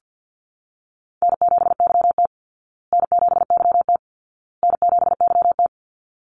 Morse code, heard as a keyed tone, about 50 wpm, 700 Hz. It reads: N7YT N7YT N7YT
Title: Call Signs Repeated 2x - 50wpm